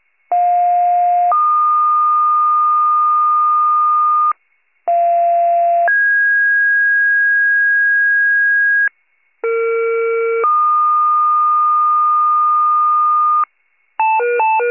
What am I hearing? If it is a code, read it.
Two-tone page, heard as a one-second tone, then a three-second tone, about 710 then 1220 Hz, about 690 then 1650 Hz, about 460 then 1220 Hz. It is heard from 0.3 to 13.4 s.
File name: wcalert